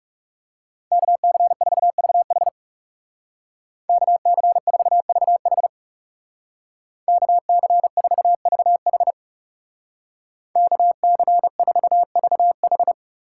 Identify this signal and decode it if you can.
Morse code, heard as a keyed tone, about 38 wpm, 700 Hz. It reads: KC4VH KC4VH KC4VH KC4VH